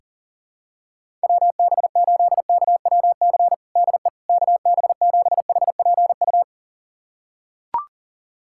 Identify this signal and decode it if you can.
Morse code, heard as a keyed tone, about 37 wpm, 700 Hz. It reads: WB8KWC DE KB7HPU
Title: USA Callsign DE Callsign - 6 Characters - 40wpm